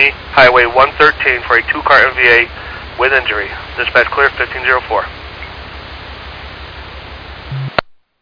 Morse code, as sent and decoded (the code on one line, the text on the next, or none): none